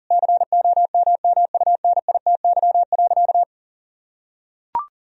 Morse code, heard as a keyed tone, about 40 wpm, 700 Hz. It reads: COMMUNITY.
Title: Sentences from Top 1000 Words - Review Part 4 - 40wpm